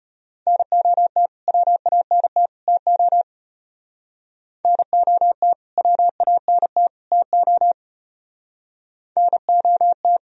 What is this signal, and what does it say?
Morse code, heard as a keyed tone, about 30 wpm, 700 Hz. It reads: NOT WANT TO NOT WANT TO NOT
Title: Speed-Racing - Sets of 3 Words 25wpm